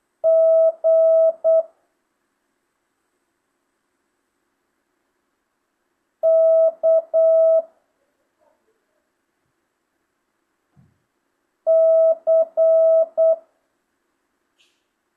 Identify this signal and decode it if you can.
Morse code, heard as a keyed tone, about 8 wpm, 650 Hz. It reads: G K C
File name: jh8cbh / 2025cw